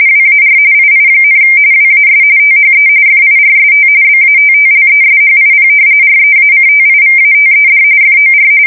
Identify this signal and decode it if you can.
RTTY, about 45.45 baud. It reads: WAR II, WAS THE FIRST BATTLESHIP REACTIVATED FOR
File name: fsk170 / battleshipnj